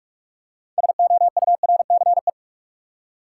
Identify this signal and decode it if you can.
Morse code, heard as a keyed tone, about 44 wpm, 700 Hz. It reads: SOURCE
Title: Six Letter Words - 45wpm